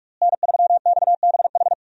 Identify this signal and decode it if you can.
Morse code, heard as a keyed tone, about 46 wpm, 700 Hz. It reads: N3XBH
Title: USA Call Signs - 5 Characters - Repeated 2x - 45wpm